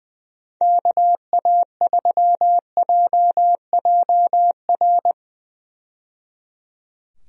Morse code, heard as a keyed tone, about 20 wpm, 700 Hz. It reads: KA3JJR
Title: USA Call Signs - 6 Characters - 20wpm